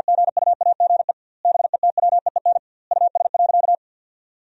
Morse code, heard as a keyed tone, about 50 wpm, 700 Hz. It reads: GUAGE BETWEEN US.
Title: 2x Word-Spacing - Sentences from Top 500 Words - 50wpm